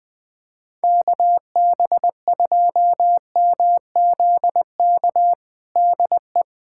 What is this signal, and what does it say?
Morse code, heard as a keyed tone, about 20 wpm, 700 Hz. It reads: KB2MZK DE
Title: USA Callsign DE Callsign 20wpm